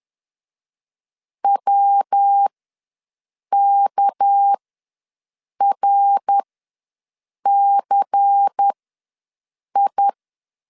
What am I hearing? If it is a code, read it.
Morse code, heard as a keyed tone, about 10 wpm, spaced as for 6 wpm, 800 Hz. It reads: WKRCI
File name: LEZIONI / 6 PKX0 / 6 PK ws 10wpm 3745